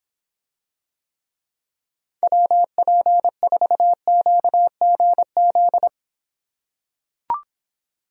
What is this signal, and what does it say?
Morse code, heard as a keyed tone, about 25 wpm, 700 Hz. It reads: WP4QGZ